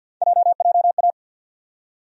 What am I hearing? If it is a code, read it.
Morse code, heard as a keyed tone, about 52 wpm, 700 Hz. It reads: JJA
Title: USA Call Sign Postfixes - 3 Letters 50wpm